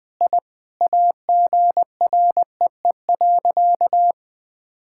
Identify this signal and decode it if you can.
Morse code, heard as a keyed tone, about 21 wpm, 700 Hz. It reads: I AGREE.
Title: Sentences with 5-Word Phrases - 20wpm